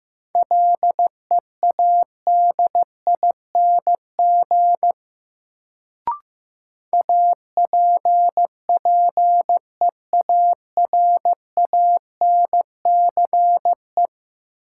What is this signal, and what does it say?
Morse code, heard as a keyed tone, about 15 wpm, 700 Hz. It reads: LEADING APPEARANCE
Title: Sentences from Top 1100 Words - New Word Drill Alt - 15wpm